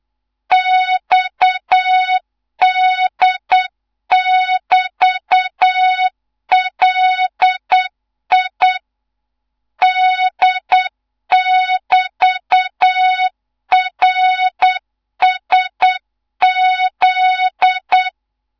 Morse code, heard as a keyed tone, about 8 wpm, 3700 Hz, with noise. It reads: XD<BT>LI D<BT>RSZ
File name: Morze-HA2MN / hang / 15 GYAKORLAT